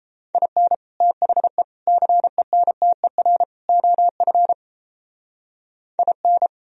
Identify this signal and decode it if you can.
Morse code, heard as a keyed tone, about 33 wpm, 700 Hz. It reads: IN THE CENTER OF IN
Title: Speed-Racing - Sets of 4 Words 28wpm